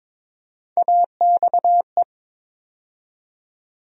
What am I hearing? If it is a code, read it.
Morse code, heard as a keyed tone, about 22 wpm, 700 Hz. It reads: AXE
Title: USA Call Sign Postfixes - 3 Letters 22wpm